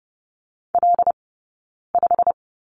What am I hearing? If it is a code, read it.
Morse code, heard as a keyed tone, about 30 wpm, spaced as for 9 wpm, 690 Hz. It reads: L5
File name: course / 30:10wpm / 086 Lesson 15d - Number 5 - Two Characters 30wpm